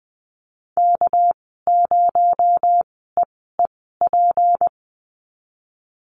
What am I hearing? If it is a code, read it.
Morse code, heard as a keyed tone, about 20 wpm, spaced as for 15 wpm, 700 Hz. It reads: K0EEP
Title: 219 Lesson 35g - Number 0 - Call Signs 20wpm